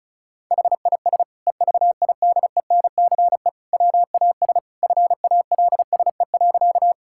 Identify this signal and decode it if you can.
Morse code, heard as a keyed tone, about 36 wpm, 700 Hz. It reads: HIS EVIDENCE WAS FALSE.
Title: Sentences from Top 1600 Words 35wpm